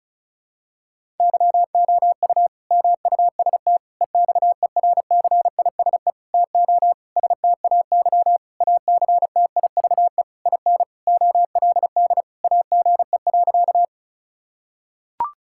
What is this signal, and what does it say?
Morse code, heard as a keyed tone, about 34 wpm, 700 Hz. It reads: YOU MUST EXERCISE TO STAY ACTIVE IN OLD AGE.
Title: Sentences from Top 1000 Words - Review Part 6 - 35wpm